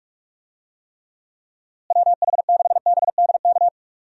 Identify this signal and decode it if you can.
Morse code, heard as a keyed tone, about 42 wpm, 700 Hz. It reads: WH6BDK
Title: USA Call Signs - 6 Characters - Repeated 2x - 45wpm